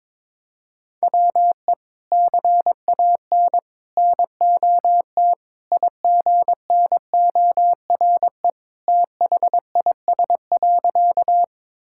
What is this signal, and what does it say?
Morse code, heard as a keyed tone, about 22 wpm, 700 Hz. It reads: WE CAN NOT IGNORE THIS.
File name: practice / 22wpm / Sentences with 3-Word Phrases - 22wpm